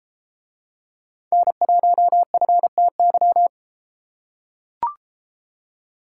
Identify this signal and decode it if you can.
Morse code, heard as a keyed tone, about 32 wpm, 700 Hz. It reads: N1FTY